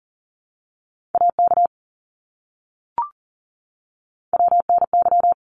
Morse code, heard as a keyed tone, about 40 wpm, 700 Hz. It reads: AK WNY
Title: ARRL Sections 40wpm